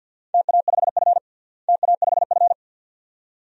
Morse code, heard as a keyed tone, about 49 wpm, 700 Hz. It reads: TA5F TA5F
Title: DXCC-100 Call Signs - 4 Characters - Repeated 2x - 50wpm